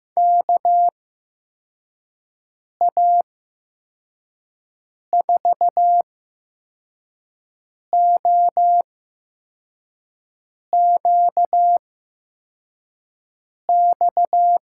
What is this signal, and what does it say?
Morse code, heard as a keyed tone, about 15 wpm, spaced as for 4 wpm, 700 Hz. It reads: KA4OQX
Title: USA Call Signs - 6 Characters - 8x Character Spacing - 15wpm